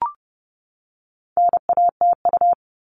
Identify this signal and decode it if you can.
Morse code, heard as a keyed tone, about 30 wpm, 700 Hz. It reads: NATU
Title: All Cut Numbers - 4-Digit 30wpm